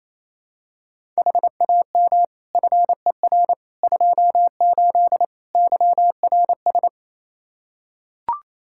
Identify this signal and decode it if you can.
Morse code, heard as a keyed tone, about 28 wpm, 700 Hz. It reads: HAM FER 28 YRS
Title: Practice QSO - Exchange 3 - More Info - 28wpm